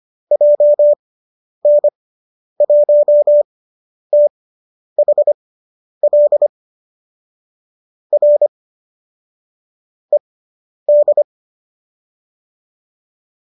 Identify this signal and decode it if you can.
Morse code, heard as a keyed tone, about 25 wpm, spaced as for 10 wpm, 570 Hz. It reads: JN1THL R ED